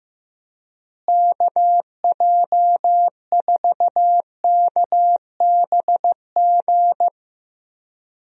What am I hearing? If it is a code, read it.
Morse code, heard as a keyed tone, about 15 wpm, 700 Hz. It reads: KJ4KBG
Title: USA Call Signs - 6 Characters - Repeated 2x - 15wpm